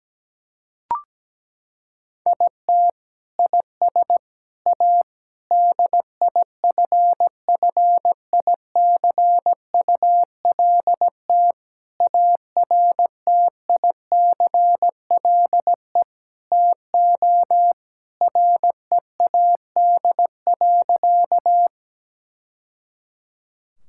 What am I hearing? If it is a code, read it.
Morse code, heard as a keyed tone, about 17 wpm, 700 Hz. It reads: IT IS A DIFFICULT ARTICLE TO READ.
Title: Sentences from Top 1000 Words - Review Part 5 - 17wpm